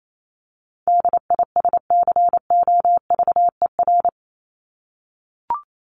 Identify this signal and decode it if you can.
Morse code, heard as a keyed tone, about 28 wpm, 700 Hz. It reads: DISCOVER
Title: Top 1001-1250 Words 28wpm